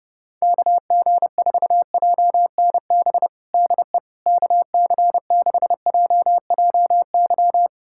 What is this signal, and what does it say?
Morse code, heard as a keyed tone, about 32 wpm, 700 Hz. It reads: KG4JNB DE KC6JJY
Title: USA Callsign DE Callsign 30wpm